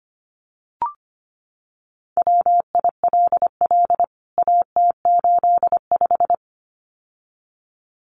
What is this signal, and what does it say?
Morse code, heard as a keyed tone, about 25 wpm, 700 Hz. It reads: WILL AT85
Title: CWT Exchange 25wpm